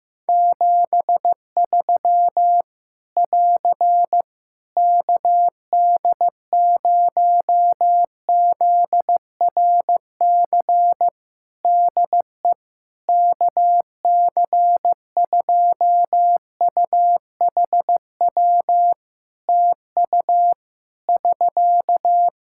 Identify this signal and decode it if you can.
Morse code, heard as a keyed tone, about 15 wpm, 700 Hz. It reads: 73 <AR> KD0ZRC DE KC2UHW TU <SK>
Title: Practice QSO - Exchange 4 - Ending - 15wpm